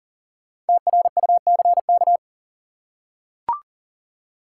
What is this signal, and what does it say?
Morse code, heard as a keyed tone, about 39 wpm, 700 Hz. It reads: TRUCK
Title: Five Letter Words - 40wpm